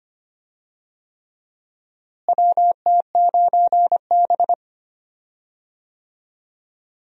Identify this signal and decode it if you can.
Morse code, heard as a keyed tone, about 25 wpm, 700 Hz. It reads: WT9B
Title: USA Call Signs - 4 Characters - Repeated 3x - 25wpm